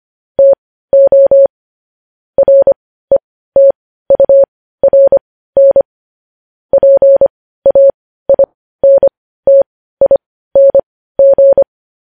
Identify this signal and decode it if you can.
Morse code, heard as a keyed tone, about 25 wpm, spaced as for 15 wpm, 550 Hz. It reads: TO RETURN PAINTING
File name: podcast / News headlines-1x-25x15